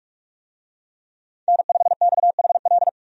Morse code, heard as a keyed tone, about 45 wpm, 700 Hz. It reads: N5XHL
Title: USA Call Signs - 5 Characters - 45wpm